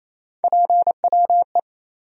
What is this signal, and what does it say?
Morse code, heard as a keyed tone, about 28 wpm, 700 Hz. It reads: PWE
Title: USA Call Sign Postfixes - 3 Letters 28wpm